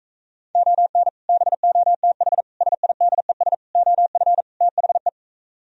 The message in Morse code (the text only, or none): ON BOTH SIDES OF THE